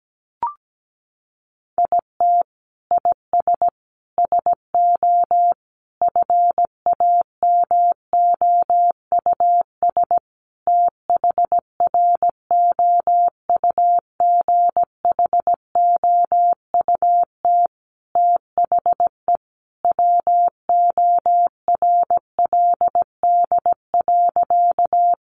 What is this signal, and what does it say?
Morse code, heard as a keyed tone, about 17 wpm, 700 Hz. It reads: IT IS SO FAMOUS THROUGHOUT THE WORLD.
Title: Sentences from Top 1500 Words 17wpm